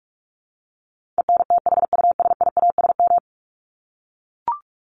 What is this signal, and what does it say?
Morse code, heard as a keyed tone, about 44 wpm, 700 Hz. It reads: ENTHUSIASM
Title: Top 1751-2000 Words 45wpm